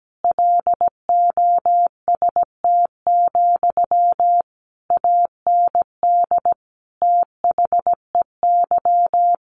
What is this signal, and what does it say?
Morse code, heard as a keyed tone, about 17 wpm, 700 Hz. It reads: LOST, AND THEY
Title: A Sound Of Thunder 17wpm